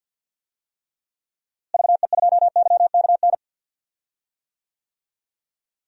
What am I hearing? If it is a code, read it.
Morse code, heard as a keyed tone, about 51 wpm, 700 Hz. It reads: VE2YKN
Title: DXCC-100 Call Signs - 6 Characters - Repeated 3x - 50wpm